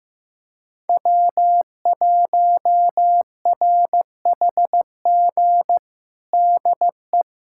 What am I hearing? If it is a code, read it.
Morse code, heard as a keyed tone, about 15 wpm, 700 Hz. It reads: W1RHG DE
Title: USA Callsign DE Callsign - 5 Characters - 15wpm